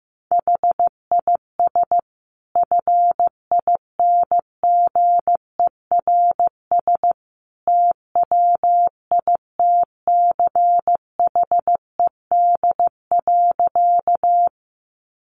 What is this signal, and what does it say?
Morse code, heard as a keyed tone, about 15 wpm, 700 Hz. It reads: HIS FINGERS TWITCHED.
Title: A Sound Of Thunder 15wpm